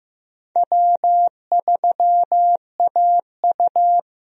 Morse code, heard as a keyed tone, about 15 wpm, 700 Hz. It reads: W3AU
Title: USA Call Signs - 4 Characters - 15wpm